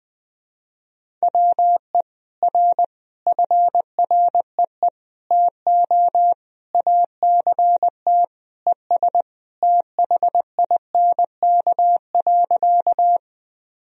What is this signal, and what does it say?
Morse code, heard as a keyed tone, about 20 wpm, 700 Hz. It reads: WE R FREE TO ACT ES THINK.
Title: Sentences from Top 500 Words and Abbreviations 20wpm